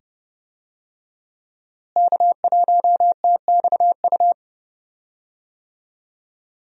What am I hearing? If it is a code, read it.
Morse code, heard as a keyed tone, about 30 wpm, 700 Hz. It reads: K1TXU
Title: USA Call Signs - 5 Characters - Repeated 2x - 30wpm